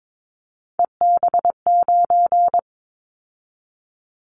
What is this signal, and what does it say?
Morse code, heard as a keyed tone, about 22 wpm, 700 Hz. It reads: EB9
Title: Callsign Prefixes for Top 100 DXCC - 22wpm